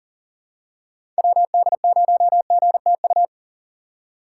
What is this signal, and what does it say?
Morse code, heard as a keyed tone, about 40 wpm, 700 Hz. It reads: WD0GTU
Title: USA Call Signs - 6 Characters - 3x Character Spacing - 40wpm